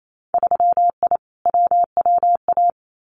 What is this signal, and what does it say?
Morse code, heard as a keyed tone, about 28 wpm, 700 Hz. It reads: 3I WWA
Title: Winter Field Day Exchanges Encore 28wpm